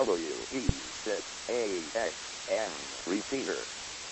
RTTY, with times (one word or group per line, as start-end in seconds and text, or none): none